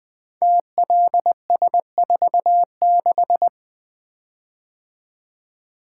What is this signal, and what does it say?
Morse code, heard as a keyed tone, about 20 wpm, 700 Hz. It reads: TLS46
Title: Code Groups - 5 Letter:Numbers - 20wpm